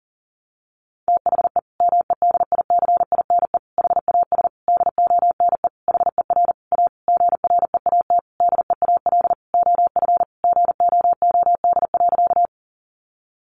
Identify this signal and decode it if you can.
Morse code, heard as a keyed tone, about 40 wpm, 700 Hz. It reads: THE MEDICINE HAS DONE HER A GREAT DEAL OF GOOD.
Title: Sentences from Top 900 Words 40wpm